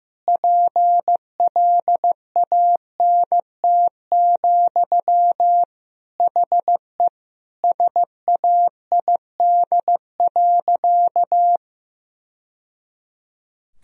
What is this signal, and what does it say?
Morse code, heard as a keyed tone, about 15 wpm, 700 Hz. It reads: PLANT, HE SAID.
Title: Sentences from Top 1000 Words - Review Part 1 - 15wpm